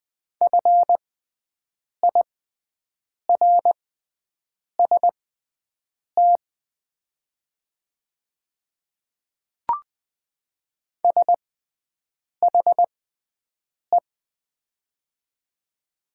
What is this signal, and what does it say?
Morse code, heard as a keyed tone, about 20 wpm, spaced as for 7 wpm, 700 Hz. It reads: FIRST SHE
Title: 6x Character Spacing - Top 200 Words - 20wpm